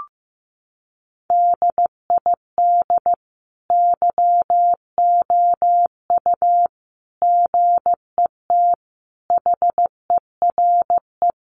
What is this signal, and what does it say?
Morse code, heard as a keyed tone, about 15 wpm, 700 Hz. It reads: DID YOU GET HERE
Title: Sentences from Top 700 Words 15wpm